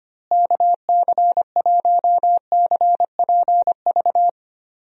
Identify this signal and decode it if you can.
Morse code, heard as a keyed tone, about 25 wpm, 700 Hz. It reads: KC1CPV
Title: USA Call Signs - 6 Characters - Repeated 3x - 25wpm